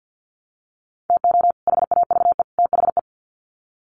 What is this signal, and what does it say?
Morse code, heard as a keyed tone, about 50 wpm, 700 Hz. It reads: TO HAVE THE
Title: Sets of 3 Words 50wpm